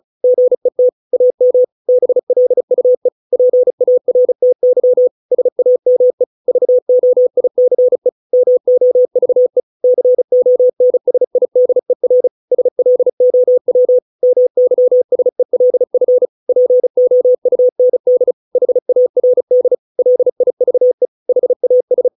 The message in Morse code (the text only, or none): GET AM BLUE PARTY SAME VOICE MOVE CONSIDER SLOW MYSELF POUND HARD LIVE HAS